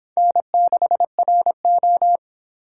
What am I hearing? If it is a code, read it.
Morse code, heard as a keyed tone, about 28 wpm, 700 Hz. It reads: N6RO